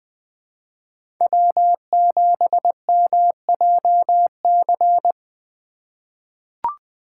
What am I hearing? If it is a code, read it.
Morse code, heard as a keyed tone, about 20 wpm, 700 Hz. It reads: W7MJC
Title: USA Call Signs - 5 Characters - 3x Character Spacing - 20wpm